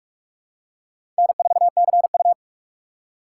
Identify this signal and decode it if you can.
Morse code, heard as a keyed tone, about 47 wpm, 700 Hz. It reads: N4CU
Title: USA Call Signs - 4 Characters - 3x Character Spacing - 45wpm